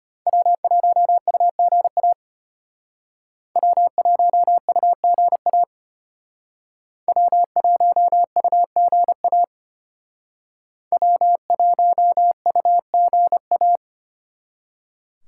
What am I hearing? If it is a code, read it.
Morse code, heard as a keyed tone, about 32 wpm, 700 Hz. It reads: W1UGA W1UGA W1UGA W1UGA